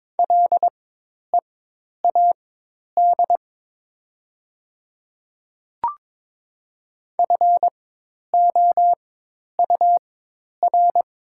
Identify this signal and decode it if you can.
Morse code, heard as a keyed tone, about 21 wpm, spaced as for 10 wpm, 700 Hz. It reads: LEAD FOUR
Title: 4x Character Spacing - Top 500 Words - 22wpm